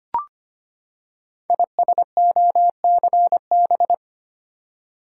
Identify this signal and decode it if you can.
Morse code, heard as a keyed tone, about 25 wpm, 700 Hz. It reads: ISOCB
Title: Code Groups - 5 Letters - 25wpm